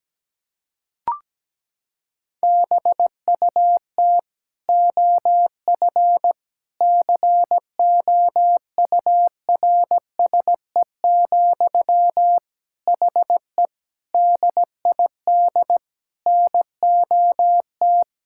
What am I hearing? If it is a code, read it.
Morse code, heard as a keyed tone, about 17 wpm, 700 Hz. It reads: BUT OF COURSE, HE DID NOT
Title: Sentences from Top 1600 Words 17wpm